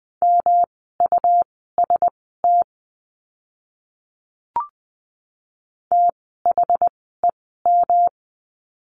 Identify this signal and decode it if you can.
Morse code, heard as a keyed tone, about 20 wpm, spaced as for 14 wpm, 700 Hz. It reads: MUST THEM
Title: 237 Lesson 38f - Letter q - Words 20wpm